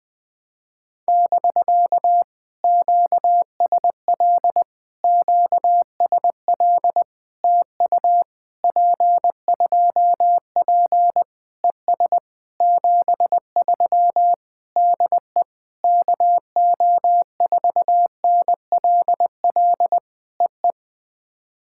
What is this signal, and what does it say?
Morse code, heard as a keyed tone, about 20 wpm, 700 Hz. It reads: <BK> QSL QSL TU P2P ES 73 DE KO4NLL EE
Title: POTA QSO Practice - 20wpm